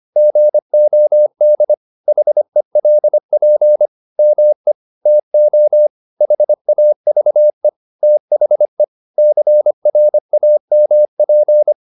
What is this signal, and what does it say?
Morse code, heard as a keyed tone, about 25 wpm, 600 Hz. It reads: GOD HELP ME TO HAVE THE CRAMP
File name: the old man and the sea part 1 25wpm